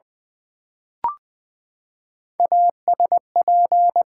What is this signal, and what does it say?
Morse code, heard as a keyed tone, about 20 wpm, 700 Hz. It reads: ASP
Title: Build A Word - 3 Letters At A Time - 6-Letter Words - 20wpm